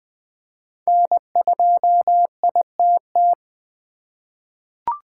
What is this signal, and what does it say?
Morse code, heard as a keyed tone, about 21 wpm, 700 Hz. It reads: N2ITT
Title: USA Call Signs - 5 Characters - 3x Character Spacing - 20wpm